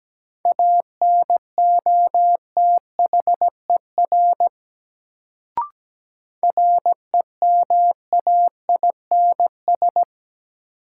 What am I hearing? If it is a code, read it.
Morse code, heard as a keyed tone, about 17 wpm, 700 Hz. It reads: ANOTHER REMAINS
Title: Sets of 5 Words Encore - New Word Drill - 17wpm